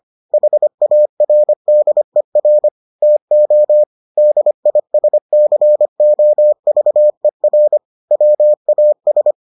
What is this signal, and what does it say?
Morse code, heard as a keyed tone, about 25 wpm, 600 Hz. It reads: HARDER TO DISCOVER WAS